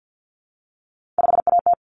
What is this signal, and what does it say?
Morse code, heard as a keyed tone, about 50 wpm, 700 Hz. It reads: 5AT
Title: Common Cut Numbers - 3-Digit 50wpm